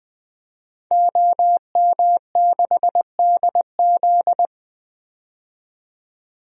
Morse code, heard as a keyed tone, about 20 wpm, 700 Hz. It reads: OM6DZ
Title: DXCC-100 Call Signs - 5 Characters - 20wpm